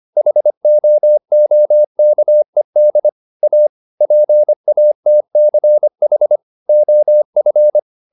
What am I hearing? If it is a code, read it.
Morse code, heard as a keyed tone, about 25 wpm, 600 Hz. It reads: HOOKED A PATCH OF